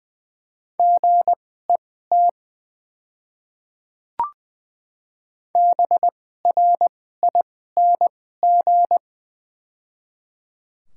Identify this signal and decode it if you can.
Morse code, heard as a keyed tone, about 20 wpm, spaced as for 14 wpm, 700 Hz. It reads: GET BRING
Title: 2x Character Spacing - Top 500 Words - 20wpm